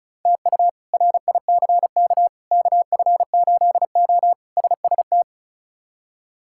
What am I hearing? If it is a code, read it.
Morse code, heard as a keyed tone, about 36 wpm, 700 Hz. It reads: TU RICK KF8O SST
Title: K1USN SST QSO Practice - 35wpm